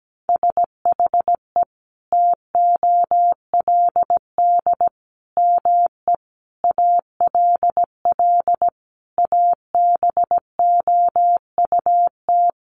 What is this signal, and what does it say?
Morse code, heard as a keyed tone, about 17 wpm, 700 Hz. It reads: SHE TOLD ME ALL ABOUT
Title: Sentences from Top 400 Words 17wpm